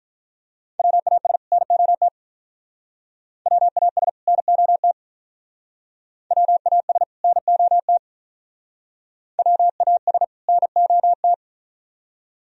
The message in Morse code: WAS NOT WAS NOT WAS NOT WAS NOT

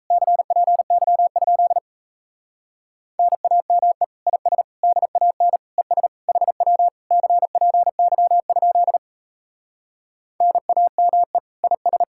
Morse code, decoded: CPY? NAME IS DAN ES HW CPY? NAME IS

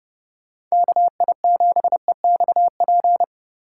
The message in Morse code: KI7EXP

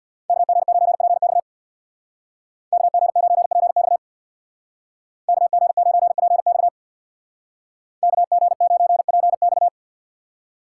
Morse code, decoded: KG9PX KG9PX KG9PX KG9PX

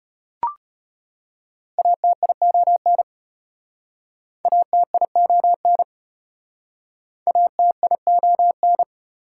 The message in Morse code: ATION ATION ATION

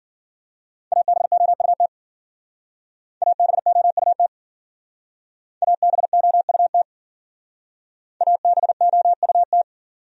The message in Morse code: ABOUT ABOUT ABOUT ABOUT

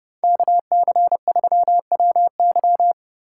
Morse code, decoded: KC3WY

